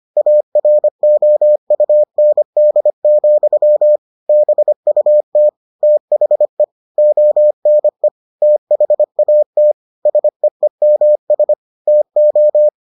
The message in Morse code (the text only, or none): AROUND, BUT THE ONE THAT SEEMS TO